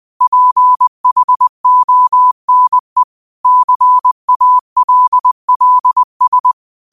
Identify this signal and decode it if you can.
Morse code, heard as a keyed tone, about 20 wpm, 1000 Hz. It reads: PHONE CALLS
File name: qotdfiles / 20wpm